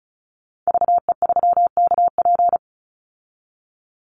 Morse code, heard as a keyed tone, about 34 wpm, 700 Hz. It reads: VE3KP
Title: Call Signs Repeated 3x - 35wpm